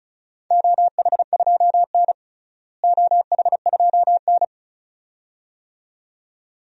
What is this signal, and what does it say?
Morse code, heard as a keyed tone, about 36 wpm, 700 Hz. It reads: OH2N OH2N
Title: DXCC-100 Call Signs - 4 Characters - Repeated 2x - 35wpm